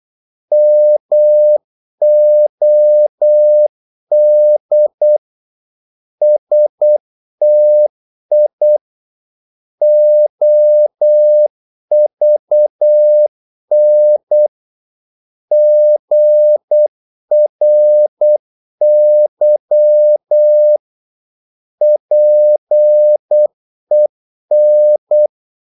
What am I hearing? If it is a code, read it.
Morse code, heard as a keyed tone, about 8 wpm, 600 Hz. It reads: MOD STI OVN GRY PEN
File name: Korte ord DK 8wpm